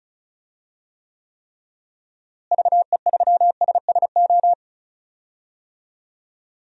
Morse code, heard as a keyed tone, about 36 wpm, 700 Hz. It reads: VE3SSO
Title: DXCC-100 Call Signs - 6 Characters - Repeated 3x - 35wpm